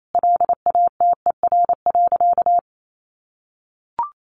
Morse code, heard as a keyed tone, about 28 wpm, 700 Hz. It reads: LATER.
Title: Sentences from Top 900 Words 28wpm